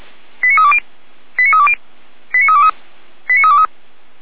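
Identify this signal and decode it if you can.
Five-tone selective call (ZVEI), heard as quick runs of five tones, tones of about 70 ms. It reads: 89320 89320 89323 89323
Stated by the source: ZVEI five-tone selective call received by SDR, Feuerwehr Orlamünde.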